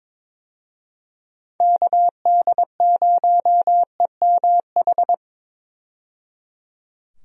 Morse code, heard as a keyed tone, about 22 wpm, 700 Hz. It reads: KD0EMH
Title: USA Call Signs - 6 Characters - 22wpm